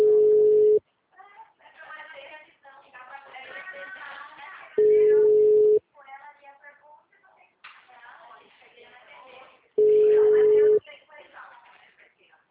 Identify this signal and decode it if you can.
Ringback tone (Europe, 425 Hz), with speech.
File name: out- -2001- - - .68533